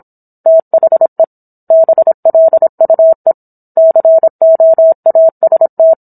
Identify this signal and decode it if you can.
Morse code, heard as a keyed tone, about 27 wpm, 650 Hz. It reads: THE BLUE COAST
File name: Book One / 17-Chapter 17 - 26 WPM